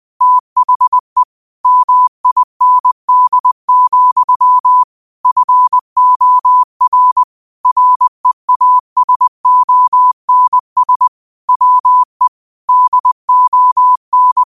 Morse code, heard as a keyed tone, about 20 wpm, 1000 Hz. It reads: THE MIND, FOR REASONS WE DON